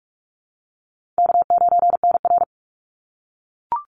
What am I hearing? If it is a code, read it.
Morse code, heard as a keyed tone, about 46 wpm, 700 Hz. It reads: K9NR
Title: Easy Call Signs - 45wpm